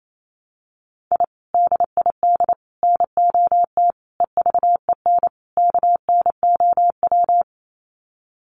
Morse code, heard as a keyed tone, about 28 wpm, 700 Hz. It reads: I DID NOT EVEN KNOW
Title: Sets of 5 Words Encore 28wpm